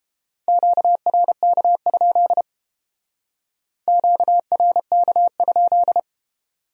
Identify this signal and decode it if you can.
Morse code, heard as a keyed tone, about 32 wpm, 700 Hz. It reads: QRK? QRK?